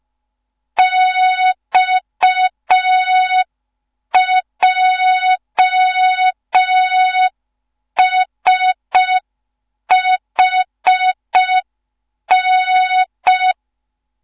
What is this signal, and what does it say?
Morse code, heard as a keyed tone, about 5 wpm, 3700 Hz. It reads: XJSHN
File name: Morze-HA2MN / hang / 15 GYAKORLAT